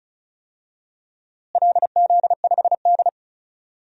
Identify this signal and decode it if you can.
Morse code, heard as a keyed tone, about 34 wpm, 700 Hz. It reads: LZ5D
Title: DXCC-100 Call Signs - 4 Characters - Repeated 2x - 35wpm